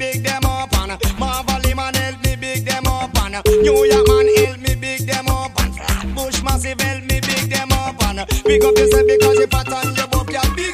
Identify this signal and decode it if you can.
Ringback tone (Europe, 425 Hz), with noise.